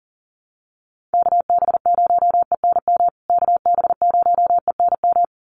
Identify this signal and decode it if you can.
Morse code, heard as a keyed tone, about 40 wpm, 700 Hz. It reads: KB0ENM KB0ENM
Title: USA Call Signs Repeated 2x 40wpm